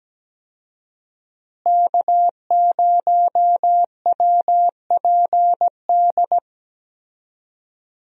Morse code, heard as a keyed tone, about 17 wpm, 700 Hz. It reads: K0WPD
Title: USA Call Signs - 5 Characters - Repeated 2x - 17wpm